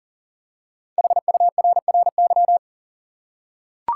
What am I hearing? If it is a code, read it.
Morse code, heard as a keyed tone, about 40 wpm, 700 Hz. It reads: HURRY
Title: Five Letter Words - 40wpm